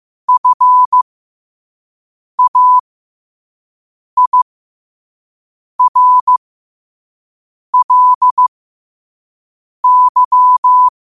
Morse code, heard as a keyed tone, about 15 wpm, spaced as for 5 wpm, 1000 Hz. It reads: FAIRLY
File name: qotdfiles / 5wpm